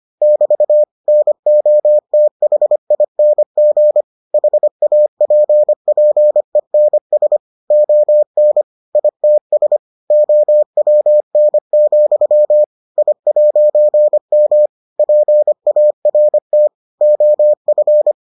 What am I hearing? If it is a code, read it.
Morse code, heard as a keyed tone, about 25 wpm, 600 Hz. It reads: <BT> NOTHING HAPPENS ON ITS OWN, I'M PART OF